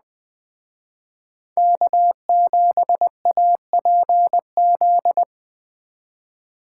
Morse code, heard as a keyed tone, about 20 wpm, 700 Hz. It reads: K7APZ
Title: Speed Racing - USA 1x3 Call Signs 17wpm